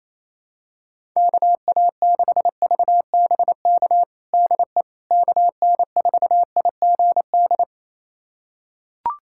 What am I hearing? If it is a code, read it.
Morse code, heard as a keyed tone, about 28 wpm, 700 Hz. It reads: KA6VBK DE KN4IGD
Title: USA Callsign DE Callsign 28wpm